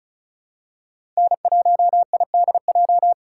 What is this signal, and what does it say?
Morse code, heard as a keyed tone, about 36 wpm, 700 Hz. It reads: N1IDJ
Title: USA Call Signs - 5 Characters - 2x Character Spacing - 35wpm